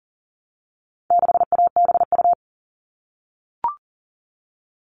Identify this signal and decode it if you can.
Morse code, heard as a keyed tone, about 40 wpm, 700 Hz. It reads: 6ABU
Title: All Cut Numbers - 4-Digit 40wpm